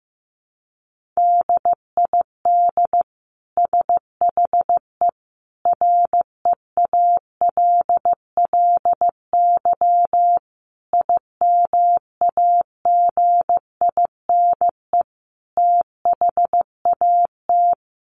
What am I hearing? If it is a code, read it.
Morse code, heard as a keyed tone, about 15 wpm, 700 Hz. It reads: DID SHE REALLY IMAGINE THAT
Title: Sentences from Top 1400 Words 15wpm